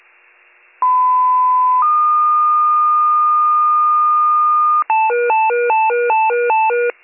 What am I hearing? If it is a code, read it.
Two-tone page, heard as a one-second tone, then a three-second tone, about 1010 then 1220 Hz. It is heard from 0.8 to 4.8 s.